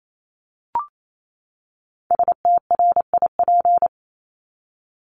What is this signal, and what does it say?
Morse code, heard as a keyed tone, about 28 wpm, 700 Hz. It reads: STRIP
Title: Top 1501-1750 Words 28wpm